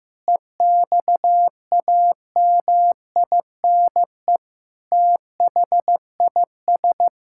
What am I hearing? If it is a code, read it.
Morse code, heard as a keyed tone, about 15 wpm, 700 Hz. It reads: EXAMINE THIS
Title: Sentences from Top 1600 Words 15wpm